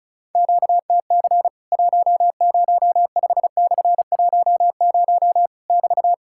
Morse code, heard as a keyed tone, about 36 wpm, 700 Hz. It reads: QTC 105/10 <BT>
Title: Worked All Europe DX Contest - QTC Exchange - 35wpm